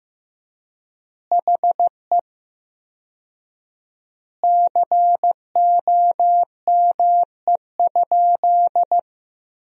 Morse code, heard as a keyed tone, about 15 wpm, 700 Hz. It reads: HE COME?
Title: 4x Word-Spacing - Sentences from Top 200 Words - 15wpm